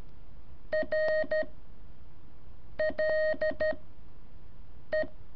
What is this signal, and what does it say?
Morse code, heard as a keyed tone, about 13 wpm, 620 Hz. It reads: R L E
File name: curso / ejercicio 2